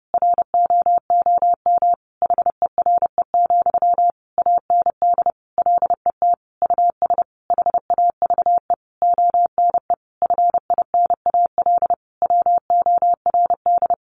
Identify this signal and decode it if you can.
Morse code, heard as a keyed tone, about 30 wpm, 700 Hz. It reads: ROOM HERE, AND LET US HAVE ONE FINAL WORD
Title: Sentences from Top 1000 Words 30wpm